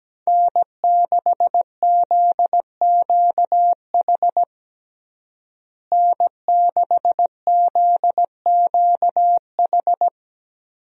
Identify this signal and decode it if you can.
Morse code, heard as a keyed tone, about 17 wpm, 700 Hz. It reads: N6ZQH N6ZQH
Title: USA Call Signs - 5 Characters - Repeated 2x - 17wpm